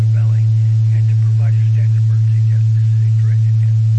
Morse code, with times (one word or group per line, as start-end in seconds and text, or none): none